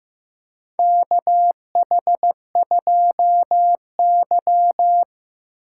Morse code, heard as a keyed tone, about 15 wpm, 700 Hz. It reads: KH2Y
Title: USA Call Signs - 4 Characters - 2x Character Spacing - 15wpm